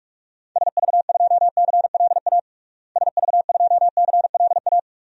Morse code, heard as a keyed tone, about 41 wpm, 700 Hz. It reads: SV2CLA SV2CLA
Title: DXCC-100 Call Signs - 6 Characters - Repeated 2x - 45wpm